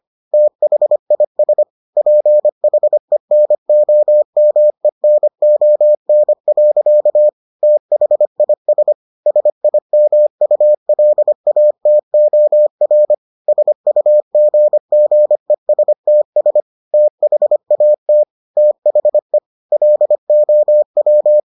Morse code, heard as a keyed tone, about 25 wpm, 600 Hz. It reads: THIS PHENOMENON. THIS SIMULATOR SUGGESTS THAT THE LOW